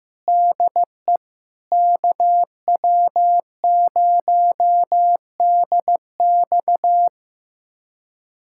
Morse code, heard as a keyed tone, about 15 wpm, 700 Hz. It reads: DE KW0DX
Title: USA Callsign DE Callsign - 5 Characters - 15wpm